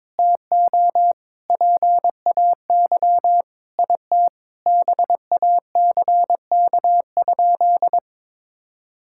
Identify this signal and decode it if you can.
Morse code, heard as a keyed tone, about 22 wpm, 700 Hz. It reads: TO PAY IT BACK?
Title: Sentences from Top 1600 Words 22wpm